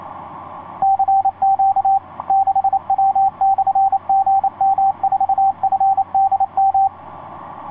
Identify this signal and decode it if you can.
Morse code, heard as a keyed tone, about 28 wpm, 780 Hz, with radio noise. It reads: CQ 6W/GM4FDM
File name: 6w-gm4fdm f6bee etc 15m cw 30jan2010 1502